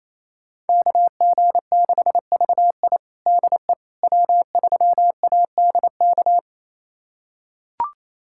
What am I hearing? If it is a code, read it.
Morse code, heard as a keyed tone, about 28 wpm, 700 Hz. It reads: KG6VI DE W3ADK
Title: USA Callsign DE Callsign - 5 Characters - 28wpm